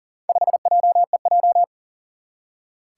Morse code, heard as a keyed tone, about 40 wpm, 700 Hz. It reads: 5JEJ
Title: Four Challenging Letter-Number Combinations 40wpm